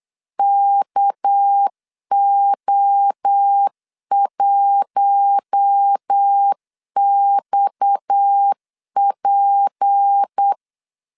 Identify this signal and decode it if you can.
Morse code, heard as a keyed tone, about 8 wpm, 800 Hz. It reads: KO1XP
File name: LEZIONI / 6 PKX0 / 6 PKX 8wpm 3041